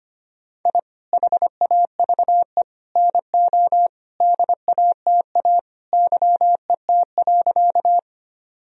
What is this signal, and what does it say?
Morse code, heard as a keyed tone, about 25 wpm, 700 Hz. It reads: I HAVE NO DATA YET.
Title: Sentences from Top 1000 Words - Review Part 3 - 25wpm